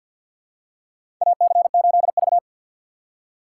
Morse code, heard as a keyed tone, about 52 wpm, 700 Hz. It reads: AC8V
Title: USA Call Signs - 4 Characters - 3x Character Spacing - 50wpm